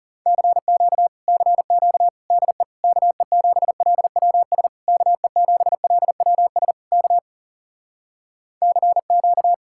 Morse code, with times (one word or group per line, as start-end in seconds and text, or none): 0.3-1.1 CQ
1.3-2.1 CQ
2.3-2.6 DE
2.8-4.7 KE7LWS
4.9-6.7 KE7LWS
6.9-7.2 K
8.6-9.5 CQ